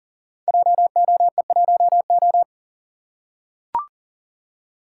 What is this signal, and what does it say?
Morse code, heard as a keyed tone, about 40 wpm, 700 Hz. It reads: JOE1O
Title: Code Groups - 5 Letter:Numbers - 40wpm